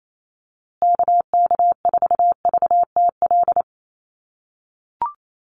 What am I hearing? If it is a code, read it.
Morse code, heard as a keyed tone, about 28 wpm, 700 Hz. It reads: KK4VTL
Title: USA Call Signs Repeated 2x 28wpm